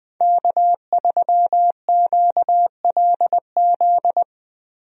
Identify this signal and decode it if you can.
Morse code, heard as a keyed tone, about 20 wpm, 700 Hz. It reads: K3QLZ
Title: USA Call Signs - 5 Characters - Repeated 2x - 20wpm